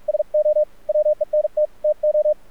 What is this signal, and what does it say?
Morse code, heard as a keyed tone, about 46 wpm, 600 Hz, with radio noise. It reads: SO WENT TO
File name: archived / CRSnet